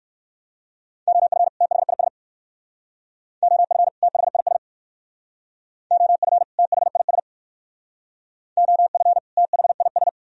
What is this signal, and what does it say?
Morse code, heard as a keyed tone, about 58 wpm, 700 Hz. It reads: OF THIS OF THIS OF THIS OF THIS